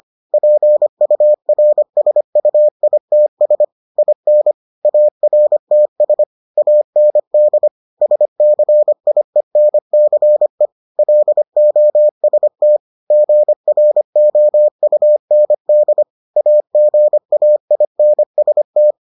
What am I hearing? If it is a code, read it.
Morse code, heard as a keyed tone, about 25 wpm, 600 Hz. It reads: PURSUITS IN ARTS AND SCIENCE LOST GROUND AGAINST